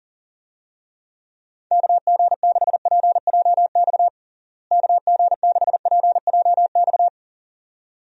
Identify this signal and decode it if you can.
Morse code, heard as a keyed tone, about 40 wpm, 700 Hz. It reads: KG6PJX KG6PJX
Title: USA Call Signs - 6 Characters - Repeated 2x - 40wpm